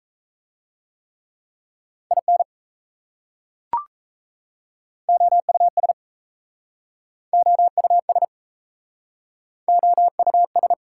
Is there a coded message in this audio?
Yes: Morse code from 2.1 to 10.7 s, heard as a keyed tone, about 36 wpm, 700 Hz. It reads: IN OUS OUS OUS